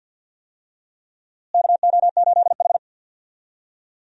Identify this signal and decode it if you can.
Morse code, heard as a keyed tone, about 53 wpm, 700 Hz. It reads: KO8H
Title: USA Call Signs - 4 Characters - Repeated 2x - 50wpm